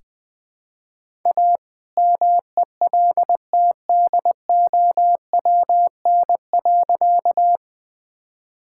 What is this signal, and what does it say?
Morse code, heard as a keyed tone, about 20 wpm, 700 Hz. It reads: A MELTDOWN.
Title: All-Age Appropriate Jokes - 20wpm